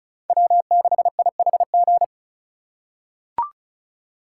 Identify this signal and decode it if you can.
Morse code, heard as a keyed tone, about 36 wpm, 700 Hz. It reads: W6IHG
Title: K1USN SST QSO Practice - 35wpm